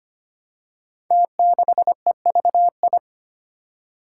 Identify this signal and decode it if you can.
Morse code, heard as a keyed tone, about 25 wpm, 700 Hz. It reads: T6EVI
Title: Five Challenging Letter-Number Combinations 25wpm